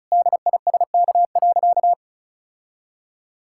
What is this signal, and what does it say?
Morse code, heard as a keyed tone, about 34 wpm, 700 Hz. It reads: DISK.
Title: Sentences from Top 1000 Words - Review Part 6 - 35wpm